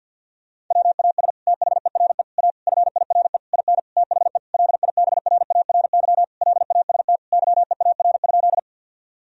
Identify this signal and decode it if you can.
Morse code, heard as a keyed tone, about 50 wpm, 700 Hz. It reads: WAS THERE A FIRE IN THE LIBRARY LAST YEAR?
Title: Sentences from Top 1000 Words - Review Part 4 - 50wpm